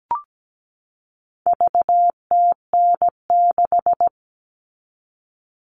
Morse code, heard as a keyed tone, about 17 wpm, 700 Hz. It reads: VTN6
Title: All Cut Numbers - 4-Digit 17wpm